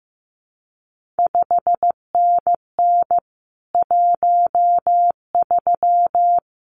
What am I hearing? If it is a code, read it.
Morse code, heard as a keyed tone, about 15 wpm, 700 Hz. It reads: 5NN 13
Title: CQ WW DX Contest Exchange - 15wpm